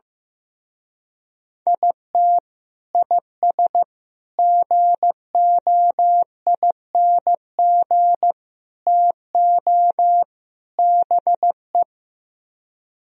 Morse code, heard as a keyed tone, about 15 wpm, 700 Hz. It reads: IT IS GOING TO BE
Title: Speed-Racing - Sets of 5 Words 15wpm